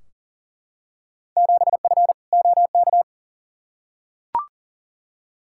Morse code, heard as a keyed tone, about 40 wpm, 700 Hz. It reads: 7F OK
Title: Field Day Exchanges Encore - 40wpm